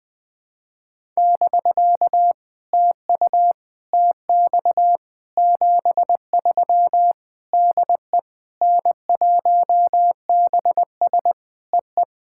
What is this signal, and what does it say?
Morse code, heard as a keyed tone, about 20 wpm, 700 Hz. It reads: <BK> TU TX 73 DE N1BS EE
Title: POTA QSO Practice - 20wpm